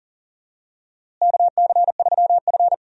Morse code, heard as a keyed tone, about 40 wpm, 700 Hz. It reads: KC3F
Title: USA Call Signs - 4 Characters - Repeated 3x - 40wpm